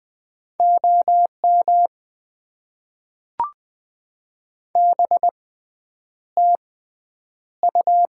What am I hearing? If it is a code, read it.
Morse code, heard as a keyed tone, about 20 wpm, spaced as for 9 wpm, 700 Hz. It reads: OM BTU
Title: Top 90 QSO Elements - 6x Character Spacing - 20wpm